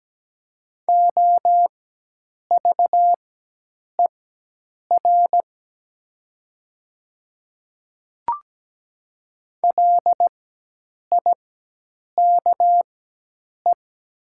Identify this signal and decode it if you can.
Morse code, heard as a keyed tone, about 17 wpm, spaced as for 8 wpm, 700 Hz. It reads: OVER LIKE
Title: 4x Character Spacing - Top 100 Words - 17wpm